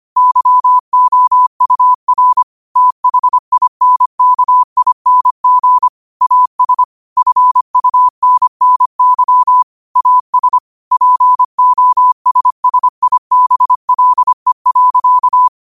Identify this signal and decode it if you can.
Morse code, heard as a keyed tone, about 25 wpm, 1000 Hz. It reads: YOUR THINKING AS FUNNY AS POSSIBLE.
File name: 25wpm